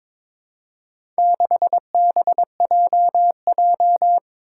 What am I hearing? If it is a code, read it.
Morse code, heard as a keyed tone, about 22 wpm, 700 Hz. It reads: 6BJJ
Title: Four Challenging Letter-Number Combinations 22wpm